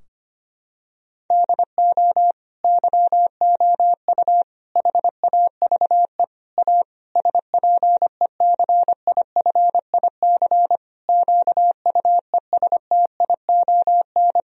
Morse code, heard as a keyed tone, about 25 wpm, 700 Hz. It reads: DO YOU HAVE A SPECIFIC QUESTION
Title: Sentences from Top 1000 Words - Review Part 3 - 25wpm